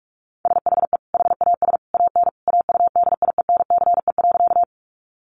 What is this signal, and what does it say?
Morse code, heard as a keyed tone, about 44 wpm, 700 Hz. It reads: SHE HAS AN AUDIENCE.
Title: Sentences from Top 800 Words 45wpm